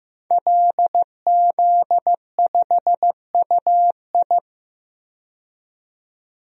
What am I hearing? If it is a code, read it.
Morse code, heard as a keyed tone, about 15 wpm, 700 Hz. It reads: LZ5UI
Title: DXCC-100 Call Signs - 5 Characters - 15wpm